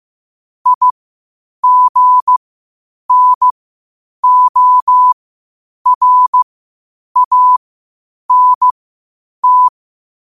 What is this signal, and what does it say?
Morse code, heard as a keyed tone, about 15 wpm, spaced as for 8 wpm, 1000 Hz. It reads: IGNORANT
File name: qotdfiles / 8wpm